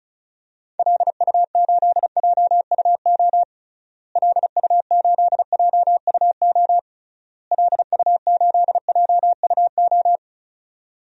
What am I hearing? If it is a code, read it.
Morse code, heard as a keyed tone, about 36 wpm, 700 Hz. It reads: LU8JUO LU8JUO LU8JUO
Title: DXCC-100 Call Signs - 6 Characters - Repeated 3x - 35wpm